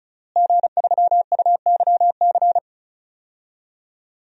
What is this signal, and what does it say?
Morse code, heard as a keyed tone, about 36 wpm, 700 Hz. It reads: G3UYC
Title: DXCC-100 Call Signs - 5 Characters - Repeated 3x - 35wpm